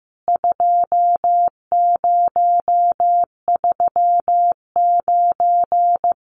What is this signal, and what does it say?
Morse code, heard as a keyed tone, about 15 wpm, 700 Hz. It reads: 2039
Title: CWT Exchange 15wpm